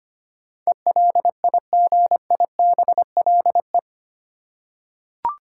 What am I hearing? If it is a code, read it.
Morse code, heard as a keyed tone, about 25 wpm, 700 Hz. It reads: ELIGIBLE
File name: Eight Letter Words - 25wpm